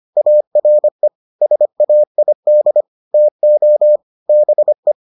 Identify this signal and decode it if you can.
Morse code, heard as a keyed tone, about 25 wpm, 600 Hz. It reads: ARE SAID TO BE